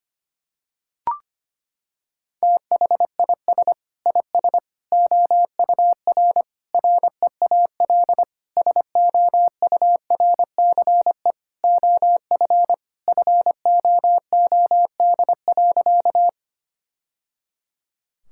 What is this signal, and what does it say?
Morse code, heard as a keyed tone, about 25 wpm, 700 Hz. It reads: THIS IS OUR REAL SOURCE OF FOOD.
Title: Sentences from Top 1000 Words - Review Part 5 - 25wpm